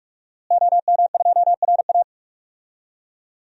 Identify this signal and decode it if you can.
Morse code, heard as a keyed tone, about 42 wpm, 700 Hz. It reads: OM2RA
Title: DXCC-100 Call Signs - 5 Characters - Repeated 2x - 45wpm